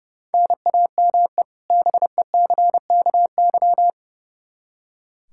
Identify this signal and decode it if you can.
Morse code, heard as a keyed tone, about 32 wpm, 700 Hz. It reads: NAME BECKY
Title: K1USN SST QSO Practice 30wpm